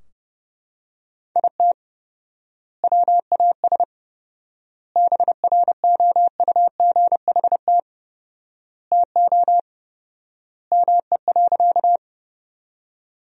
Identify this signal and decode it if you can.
Morse code, heard as a keyed tone, about 32 wpm, 700 Hz. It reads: IT WAS BROUGHT TO ME.
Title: 4x Word-Spacing - Sentences from Top 500 Words - 30wpm